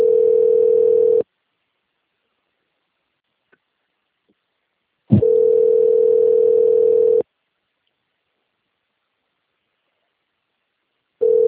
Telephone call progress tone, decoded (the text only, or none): ringback tone